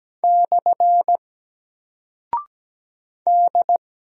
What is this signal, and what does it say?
Morse code, heard as a keyed tone, about 17 wpm, 700 Hz. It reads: / D
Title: Single Character (Letter:Number:Punctuation:Prosign) 17wpm